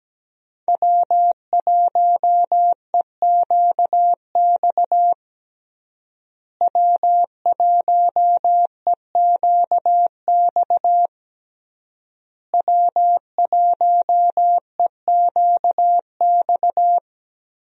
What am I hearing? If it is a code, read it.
Morse code, heard as a keyed tone, about 17 wpm, 700 Hz. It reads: W1EQX W1EQX W1EQX
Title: USA Call Signs - 5 Characters - Repeated 3x - 17wpm